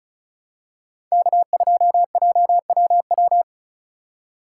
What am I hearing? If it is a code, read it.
Morse code, heard as a keyed tone, about 34 wpm, 700 Hz. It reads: K2JWW
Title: USA Call Signs - 5 Characters - 4x Character Spacing - 35wpm